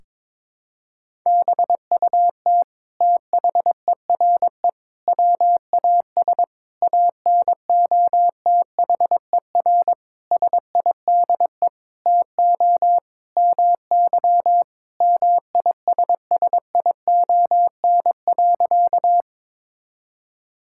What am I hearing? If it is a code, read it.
Morse code, heard as a keyed tone, about 22 wpm, 700 Hz. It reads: BUT THERE WAS ANOTHER SIDE TO MY MISSION.
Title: Sentences from Top 1600 Words 22wpm